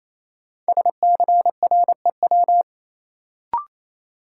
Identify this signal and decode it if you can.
Morse code, heard as a keyed tone, about 28 wpm, 700 Hz. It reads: SCREW
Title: Sentences from Top 1600 Words - New Word Drill Alt - 28wpm